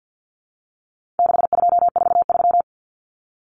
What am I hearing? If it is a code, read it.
Morse code, heard as a keyed tone, about 50 wpm, 700 Hz. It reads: 6243
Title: Numbers - 4-Digit 50wpm